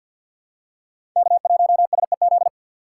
Morse code, heard as a keyed tone, about 47 wpm, 700 Hz. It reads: K1SEZ
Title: USA Call Signs - 5 Characters - 6x Character Spacing - 50wpm